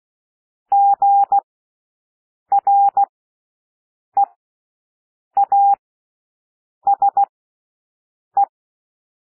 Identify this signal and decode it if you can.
Morse code, heard as a keyed tone, about 16 wpm, spaced as for 6 wpm, 800 Hz. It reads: GREASE
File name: cwmp3 / W1 SOUND WORDS 16 wpm